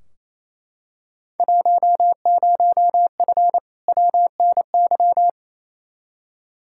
Morse code, heard as a keyed tone, about 28 wpm, 700 Hz. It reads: 10F WNY
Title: Field Day Exchanges Encore - 28wpm